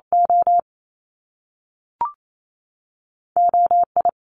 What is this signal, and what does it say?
Morse code, heard as a keyed tone, about 28 wpm, 700 Hz. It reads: O O I
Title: Top Two Letter Combinations in Words 28wpm